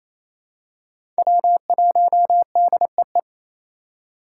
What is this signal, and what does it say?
Morse code, heard as a keyed tone, about 28 wpm, 700 Hz. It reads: W1DEE
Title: USA Call Signs - 2x Character Spacing - 28wpm